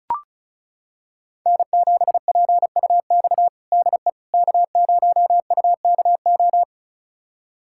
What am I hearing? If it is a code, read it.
Morse code, heard as a keyed tone, about 34 wpm, 700 Hz. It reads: N7PUX DE K0UKO
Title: USA Callsign DE Callsign - 5 Characters - 35wpm